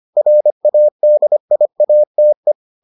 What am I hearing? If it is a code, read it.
Morse code, heard as a keyed tone, about 25 wpm, 600 Hz. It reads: RADIATE